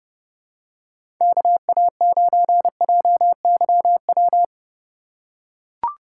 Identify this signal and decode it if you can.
Morse code, heard as a keyed tone, about 31 wpm, 700 Hz. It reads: KA9JYW
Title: USA Call Signs - 6 Characters - 2x Character Spacing - 30wpm